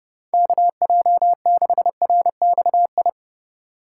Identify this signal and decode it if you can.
Morse code, heard as a keyed tone, about 29 wpm, 700 Hz. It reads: KJ6RXI